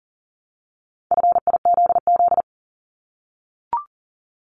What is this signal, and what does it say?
Morse code, heard as a keyed tone, about 41 wpm, 700 Hz. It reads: FIZZ
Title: Multi-letter Phonemes 40wpm